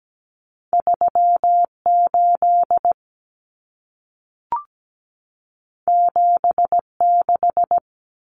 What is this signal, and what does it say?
Morse code, heard as a keyed tone, about 17 wpm, 700 Hz. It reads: 38 76
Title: Common Cut Numbers - 2-Digit 17wpm